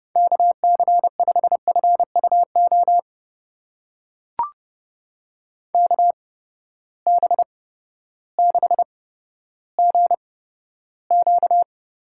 Morse code, heard as a keyed tone, about 30 wpm, spaced as for 14 wpm, 700 Hz. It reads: KC5FUO KB6GQ
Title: USA Call Signs - 6 Characters - 8x Character Spacing - 30wpm